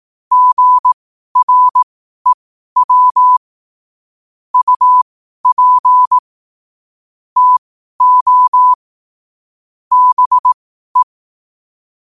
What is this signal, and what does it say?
Morse code, heard as a keyed tone, about 18 wpm, spaced as for 12 wpm, 1000 Hz. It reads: GREW UP TO BE
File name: qotdfiles / 12wpm